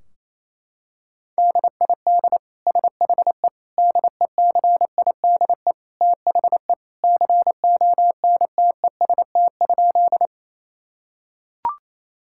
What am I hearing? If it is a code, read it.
Morse code, heard as a keyed tone, about 27 wpm, 700 Hz. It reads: DID SHE DECIDE THE CONTEST?
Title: Sentences from Top 1600 Words 28wpm